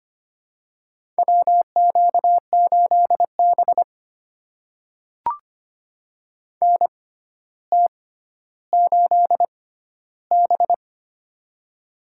Morse code, heard as a keyed tone, about 25 wpm, spaced as for 13 wpm, 700 Hz. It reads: WQ8B NT8B
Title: USA Call Signs - 4 Characters - 6x Character Spacing - 25wpm